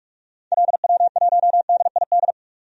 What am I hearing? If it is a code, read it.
Morse code, heard as a keyed tone, about 43 wpm, 700 Hz. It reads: LW1DID
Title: DXCC-100 Call Signs - 6 Characters - 45wpm